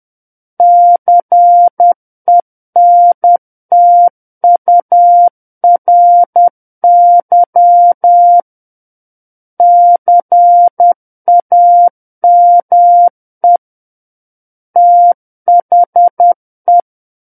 Morse code, heard as a keyed tone, about 10 wpm, 700 Hz. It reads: CENTURY CAME THE
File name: Book One / 01-Chapter 1 - 10 WPM